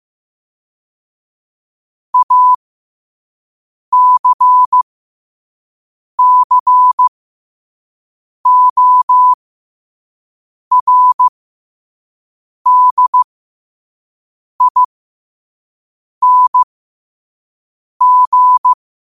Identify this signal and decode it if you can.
Morse code, heard as a keyed tone, about 15 wpm, spaced as for 5 wpm, 1000 Hz. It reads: ACCORDING
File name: qotdfiles / 5wpm